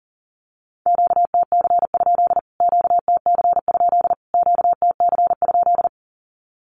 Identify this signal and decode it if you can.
Morse code, heard as a keyed tone, about 37 wpm, 700 Hz. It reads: QTC? QTC? QTC?
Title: Q-Codes Repeated 3x 40wpm